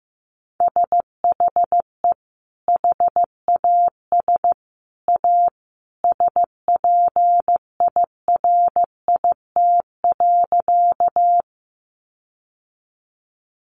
Morse code, heard as a keyed tone, about 15 wpm, 700 Hz. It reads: SHE HAS A SPIRIT.
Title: Sentences from Top 1400 Words 15wpm